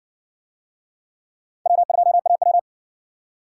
Morse code, heard as a keyed tone, about 60 wpm, 700 Hz. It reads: W2AW